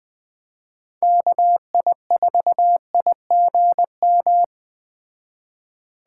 Morse code, heard as a keyed tone, about 20 wpm, 700 Hz. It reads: KI4IGM
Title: Speed Racing - USA 2x3 Call Signs 17wpm